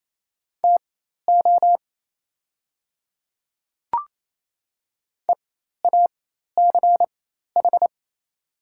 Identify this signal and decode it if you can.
Morse code, heard as a keyed tone, about 28 wpm, spaced as for 13 wpm, 700 Hz. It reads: TO EACH
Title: 4x Character Spacing - Top 100 Words - 28wpm